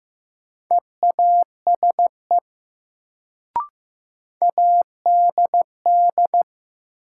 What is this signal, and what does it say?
Morse code, heard as a keyed tone, about 16 wpm, 700 Hz. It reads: EASE ADD
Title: Sentences from Top 1000 Words - Review Part 1 - New Word Drill - 15wpm